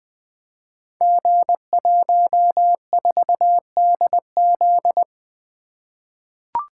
Morse code, heard as a keyed tone, about 20 wpm, 700 Hz. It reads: G14DZ
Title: Code Groups - 5 Letter:Numbers - 20wpm